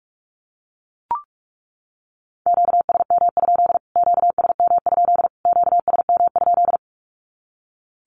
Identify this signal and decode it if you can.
Morse code, heard as a keyed tone, about 44 wpm, 700 Hz. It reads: QSM? QSM? QSM?
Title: Q-Codes Repeated 3x 45wpm